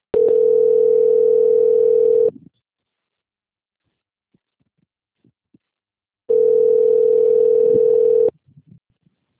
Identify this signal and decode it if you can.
Ringback tone (North America), 440 + 480 Hz.